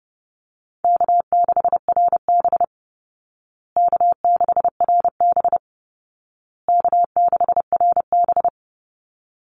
Morse code, heard as a keyed tone, about 30 wpm, 700 Hz. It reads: K6RB K6RB K6RB
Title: Easy Call Signs Repeated 3x - 30wpm